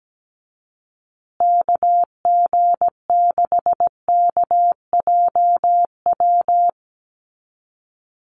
Morse code, heard as a keyed tone, about 17 wpm, 700 Hz. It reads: KG6KJW
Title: USA Call Signs Repeated 2x 17wpm